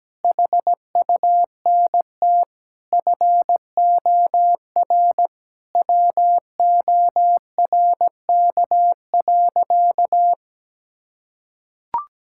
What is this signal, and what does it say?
Morse code, heard as a keyed tone, about 17 wpm, 700 Hz. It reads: HUNT FOR WORK.
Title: Sentences from Top 1600 Words 17wpm